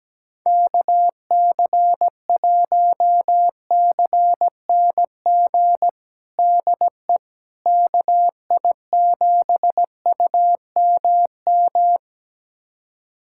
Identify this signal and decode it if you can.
Morse code, heard as a keyed tone, about 17 wpm, 700 Hz. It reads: KC1CNG DE KI7UMM
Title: USA Callsign DE Callsign 17wpm